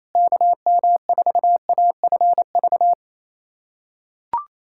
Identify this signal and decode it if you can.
Morse code, heard as a keyed tone, about 28 wpm, 700 Hz. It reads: KM4AFV
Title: USA Callsign DE Callsign - 6 Characters - 28wpm